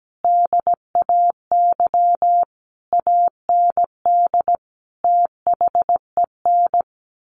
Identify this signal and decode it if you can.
Morse code, heard as a keyed tone, about 17 wpm, 700 Hz. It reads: DAY AND THEN
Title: A Sound Of Thunder 17wpm